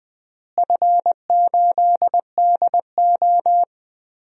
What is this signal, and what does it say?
Morse code, heard as a keyed tone, about 20 wpm, 700 Hz. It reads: F8DO
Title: DXCC-100 Call Signs - 4 Characters - Repeated 3x - 20wpm